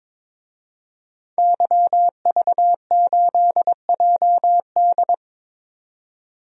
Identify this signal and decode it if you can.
Morse code, heard as a keyed tone, about 22 wpm, 700 Hz. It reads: YV8JD
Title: DXCC-100 Call Signs - 5 Characters - 22wpm